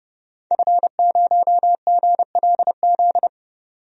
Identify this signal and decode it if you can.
Morse code, heard as a keyed tone, about 30 wpm, 700 Hz. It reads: F0GLZ
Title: DXCC-100 Call Signs - 5 Characters - 30wpm